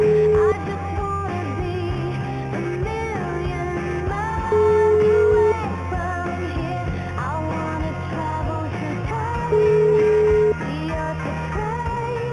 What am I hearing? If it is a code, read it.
Ringback tone (Europe, 425 Hz), with noise.